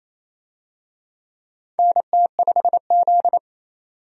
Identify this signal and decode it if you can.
Morse code, heard as a keyed tone, about 28 wpm, 700 Hz. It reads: NT5Z
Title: USA Call Signs - 4 Characters - 28wpm